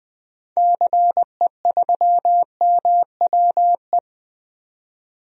Morse code, heard as a keyed tone, about 20 wpm, 700 Hz. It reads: CE3MWE
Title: DXCC-100 Call Signs - 6 Characters - 20wpm